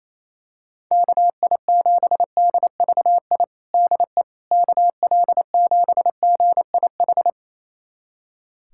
Morse code, heard as a keyed tone, about 28 wpm, 700 Hz. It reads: KI7DVI DE KL7GIH
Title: USA Callsign DE Callsign - 6 Characters - 28wpm